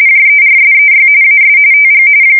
RTTY, about 45.45 baud. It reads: JERSEY AS TG
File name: fsk170 / USS-Massachusetts